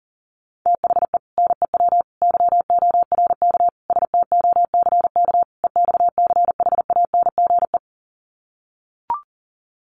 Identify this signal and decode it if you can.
Morse code, heard as a keyed tone, about 40 wpm, 700 Hz. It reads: THE NEW YORK STOCK EXCHANGE
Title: Sets of 5 Words 40wpm